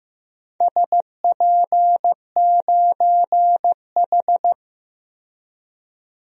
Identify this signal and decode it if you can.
Morse code, heard as a keyed tone, about 15 wpm, 700 Hz. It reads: SP9H
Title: DXCC-100 Call Signs - 4 Characters - 15wpm